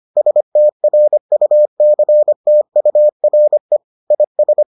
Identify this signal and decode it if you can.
Morse code, heard as a keyed tone, about 25 wpm, 600 Hz. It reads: STRUCTURE IS